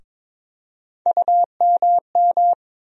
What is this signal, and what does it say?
Morse code, heard as a keyed tone, about 22 wpm, 700 Hz. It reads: UMM
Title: USA Call Sign Postfixes - 3 Letters 22wpm